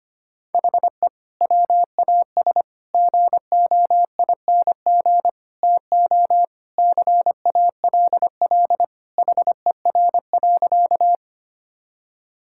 Morse code, heard as a keyed tone, about 25 wpm, 700 Hz. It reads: HE WAS GOING TO CALL HER.
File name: practice / 25wpm / Sentences with 4-Word Phrases - 25wpm